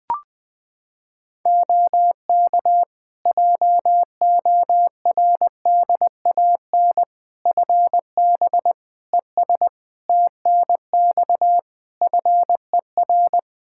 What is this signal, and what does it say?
Morse code, heard as a keyed tone, about 20 wpm, 700 Hz. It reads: OK JORDAN FB ES TNX FER
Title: Practice QSO - Exchange 2 - Info - 20wpm